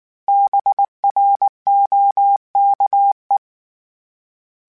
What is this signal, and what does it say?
Morse code, heard as a keyed tone, about 19 wpm, 800 Hz. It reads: BROKE
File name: Misc / Misc-mp3s / BIG @18